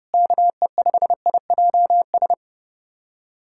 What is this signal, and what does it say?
Morse code, heard as a keyed tone, about 30 wpm, 700 Hz. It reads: KE5IJS
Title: USA Callsign DE Callsign - 6 Characters - 30wpm